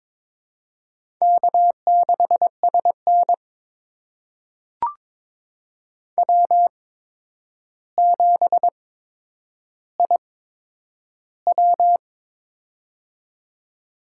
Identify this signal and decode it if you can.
Morse code, heard as a keyed tone, about 22 wpm, spaced as for 9 wpm, 700 Hz. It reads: K6SN W7IW
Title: USA Call Signs - 4 Characters - 8x Character Spacing - 22wpm